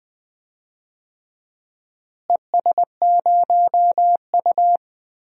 Morse code, heard as a keyed tone, about 20 wpm, 700 Hz. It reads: ES0U
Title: DXCC-100 Call Signs - 4 Characters - Repeated 3x - 20wpm